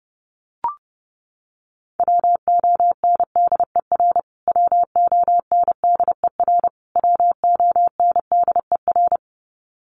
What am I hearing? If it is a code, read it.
Morse code, heard as a keyed tone, about 30 wpm, 700 Hz. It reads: WONDER WONDER WONDER
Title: Top 500 Words Repeated 30wpm